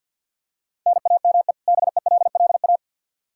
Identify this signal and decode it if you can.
Morse code, heard as a keyed tone, about 51 wpm, 700 Hz. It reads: NAME BELLA
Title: Practice QSO - Exchange 1 - Beginning - 50wpm